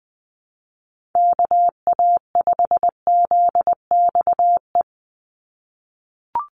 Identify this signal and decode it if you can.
Morse code, heard as a keyed tone, about 20 wpm, 700 Hz. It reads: KA5ZXE
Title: USA Call Signs 20wpm